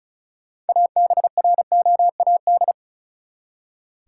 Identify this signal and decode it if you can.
Morse code, heard as a keyed tone, about 36 wpm, 700 Hz. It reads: ABROAD
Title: Six Letter Words - 35wpm